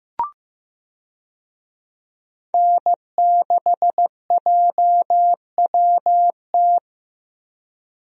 Morse code, heard as a keyed tone, about 15 wpm, 700 Hz. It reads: N6JWT
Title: USA Call Signs - 5 Characters - Repeated 2x - 15wpm